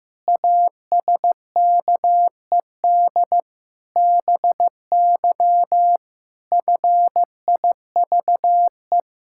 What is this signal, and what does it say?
Morse code, heard as a keyed tone, about 14 wpm, 700 Hz. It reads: ASKED BY FIVE
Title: Pangram Challenge v2 - 15wpm